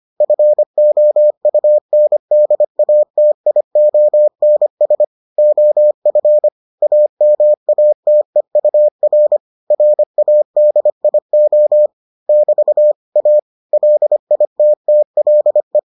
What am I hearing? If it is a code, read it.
Morse code, heard as a keyed tone, about 25 wpm, 600 Hz. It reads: FOUNDATIONS OF AMATEUR RADIO <BT> A LITTLE